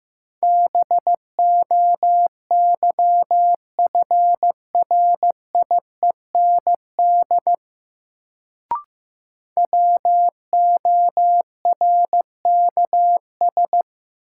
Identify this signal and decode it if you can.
Morse code, heard as a keyed tone, about 15 wpm, 700 Hz. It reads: BOYFRIEND WORKS
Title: Sentences from Top 1000 Words - Review Part 5 - New Word Drill - 15wpm